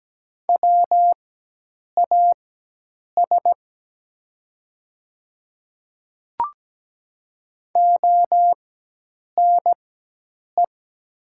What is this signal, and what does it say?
Morse code, heard as a keyed tone, about 17 wpm, spaced as for 8 wpm, 700 Hz. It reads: WAS ONE
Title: 4x Character Spacing - Top 100 Words - 17wpm